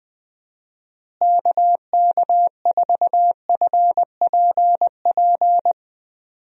Morse code, heard as a keyed tone, about 21 wpm, 700 Hz. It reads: KK4FPP
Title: USA Call Signs - 6 Characters - Repeated 3x - 20wpm